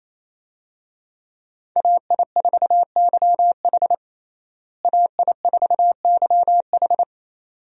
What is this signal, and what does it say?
Morse code, heard as a keyed tone, about 28 wpm, 700 Hz. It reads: AI4YH AI4YH
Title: USA Call Signs - 5 Characters - Repeated 2x - 28wpm